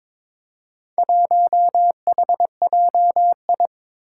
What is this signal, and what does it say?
Morse code, heard as a keyed tone, about 22 wpm, 700 Hz. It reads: 1HJI
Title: Four Challenging Letter-Number Combinations 22wpm